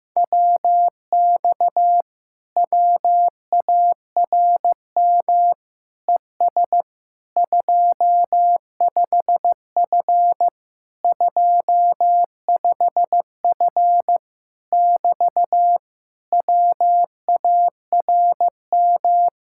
Morse code, heard as a keyed tone, about 15 wpm, 700 Hz. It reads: WX WARM ES 25F 25F <BT> WARM
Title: Speed-Racing Common QSO Phrases Encore - 15wpm